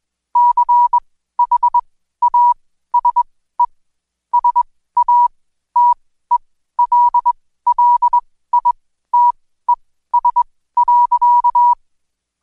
Morse code, heard as a keyed tone, about 21 wpm, spaced as for 14 wpm, 970 Hz. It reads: CHASE SATELLITES.
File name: Code Quick / AudioDownld2 / Seg123-Day28-DayDrill 28